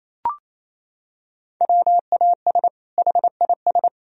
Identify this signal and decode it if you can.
Morse code, heard as a keyed tone, about 28 wpm, 700 Hz. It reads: WAS HIS
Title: Sentences from Top 1000 Words - Review Part 4 - 28wpm